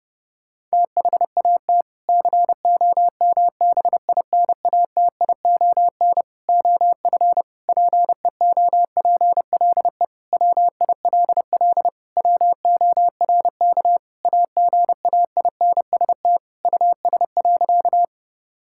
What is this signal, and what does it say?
Morse code, heard as a keyed tone, about 30 wpm, 700 Hz. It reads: THAT COMBINATION OF PEOPLE WILL WORK AGAINST US.
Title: Sentences from Top 1000 Words - Review Part 6 - 30wpm